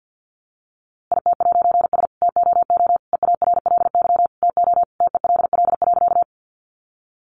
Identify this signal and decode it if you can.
Morse code, heard as a keyed tone, about 50 wpm, 700 Hz. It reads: IT'S TOO EARLY TO TELL.
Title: A Sound Of Thunder 50wpm